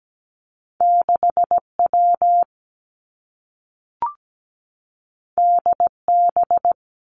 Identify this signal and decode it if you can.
Morse code, heard as a keyed tone, about 17 wpm, 700 Hz. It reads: 6W DB
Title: Two Challenging Letter-Number Combinations 17wpm